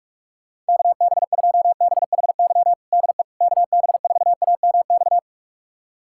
Morse code, heard as a keyed tone, about 46 wpm, 700 Hz. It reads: KB2BHY DE KB4AMX
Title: USA Callsign DE Callsign - 6 Characters - 45wpm